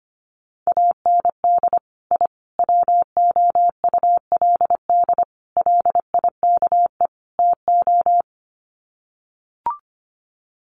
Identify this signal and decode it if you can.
Morse code, heard as a keyed tone, about 25 wpm, 700 Hz. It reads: AND I WOULD LIKE TO
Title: Sets of 5 Words Encore 25wpm